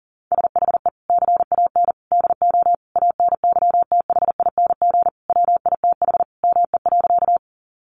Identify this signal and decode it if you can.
Morse code, heard as a keyed tone, about 40 wpm, 700 Hz. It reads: SHE CAN DO ANYTHING WITH ME.
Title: Sentences from Top 1000 Words 40wpm